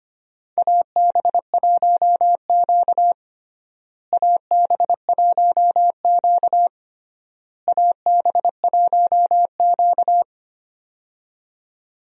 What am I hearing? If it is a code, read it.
Morse code, heard as a keyed tone, about 25 wpm, 700 Hz. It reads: AB1Q AB1Q AB1Q
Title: USA Call Signs - 4 Characters - Repeated 3x - 25wpm